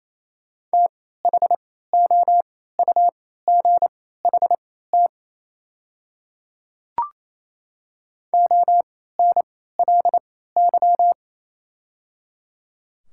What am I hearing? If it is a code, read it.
Morse code, heard as a keyed tone, about 28 wpm, spaced as for 16 wpm, 700 Hz. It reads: THOUGHT ONLY
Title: 3x Character Spacing - Top 300 Words - 28wpm